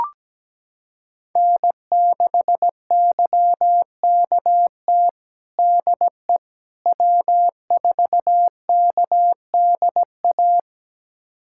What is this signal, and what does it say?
Morse code, heard as a keyed tone, about 17 wpm, 700 Hz. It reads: N6YKT DE W4KDA
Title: USA Callsign DE Callsign - 5 Characters - 17wpm